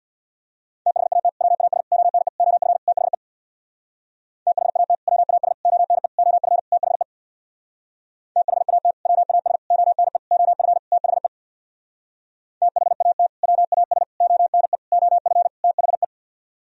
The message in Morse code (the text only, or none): THAT WAS ONE OF THE THAT WAS ONE OF THE THAT WAS ONE OF THE THAT WAS ONE OF THE